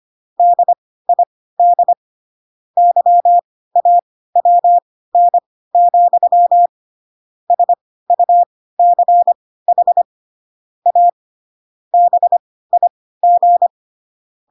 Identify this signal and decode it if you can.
Morse code, heard as a keyed tone, about 25 wpm, spaced as for 16 wpm, 700 Hz. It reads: DID YAWN, SUCH A BIG